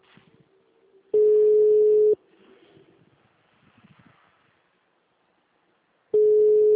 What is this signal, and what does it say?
Ringback tone (Europe, 425 Hz).